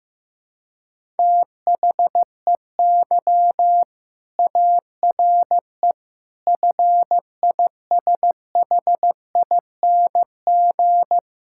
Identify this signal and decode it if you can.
Morse code, heard as a keyed tone, about 15 wpm, 700 Hz. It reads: THEY ARE FISHING
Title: Sentences from Top 1000 Words - Review Part 5 - 15wpm